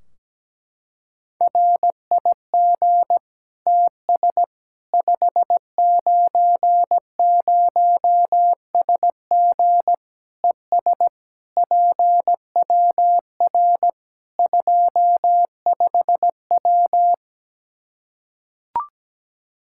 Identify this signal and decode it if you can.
Morse code, heard as a keyed tone, about 17 wpm, 700 Hz. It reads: RIG TS 590SG ES PWR 25W
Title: K1USN SST QSO Practice 17wpm